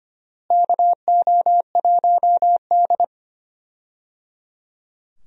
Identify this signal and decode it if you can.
Morse code, heard as a keyed tone, about 25 wpm, 700 Hz. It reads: KO1D
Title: USA Call Signs - 4 Characters - Repeated 3x - 25wpm